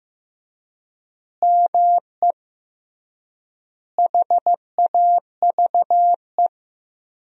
Morse code, heard as a keyed tone, about 16 wpm, 700 Hz. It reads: ME HAVE
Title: 3x Word-Spacing - Sentences from Top 300 Words - 15wpm